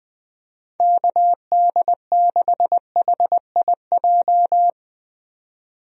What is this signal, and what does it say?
Morse code, heard as a keyed tone, about 20 wpm, 700 Hz. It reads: KD6HIJ
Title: USA Call Signs - 6 Characters - 20wpm